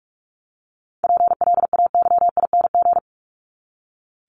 Morse code, heard as a keyed tone, about 44 wpm, 700 Hz. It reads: PLAYING
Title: ING Suffix - Top 100 Words with Base Words 45wpm